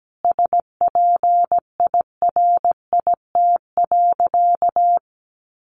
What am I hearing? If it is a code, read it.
Morse code, heard as a keyed tone, about 17 wpm, 700 Hz. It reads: SPIRIT.
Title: Sentences from Top 1400 Words 17wpm